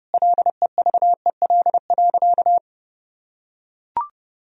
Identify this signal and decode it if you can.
Morse code, heard as a keyed tone, about 32 wpm, 700 Hz. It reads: LEVEL.
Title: Sentences from Top 1000 Words - Review Part 4 - 30wpm